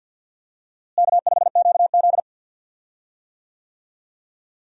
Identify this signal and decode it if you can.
Morse code, heard as a keyed tone, about 47 wpm, 700 Hz. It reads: K5QZ
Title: USA Call Signs - 4 Characters - Repeated 2x - 50wpm